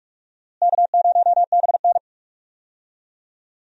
Morse code, heard as a keyed tone, about 46 wpm, 700 Hz. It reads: K0BN
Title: USA Call Signs - 4 Characters - Repeated 3x - 45wpm